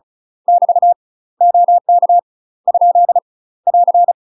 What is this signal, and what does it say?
Morse code, heard as a keyed tone, about 36 wpm, 700 Hz. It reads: <BT> OK ? <AR>
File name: UK MASTER QSOs TRAINING / 35WPM / UK MASTER QSOs 0009